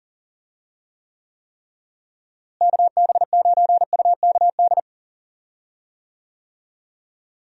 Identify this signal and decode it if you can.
Morse code, heard as a keyed tone, about 40 wpm, 700 Hz. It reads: KB9UKD
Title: USA Call Signs - 6 Characters - Repeated 3x - 40wpm